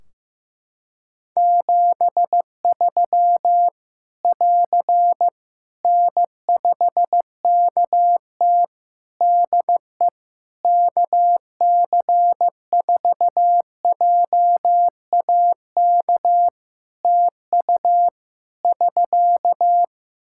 Morse code, heard as a keyed tone, about 15 wpm, 700 Hz. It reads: 73 <AR> N5KT DE KC4JAK TU <SK>
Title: K1USN SST QSO Practice 15wpm